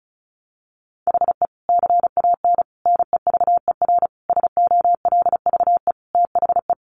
Morse code, heard as a keyed tone, about 34 wpm, 700 Hz. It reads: HE CAN NEVER SOLVE THE
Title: Sentences from Top 1300 Words 35wpm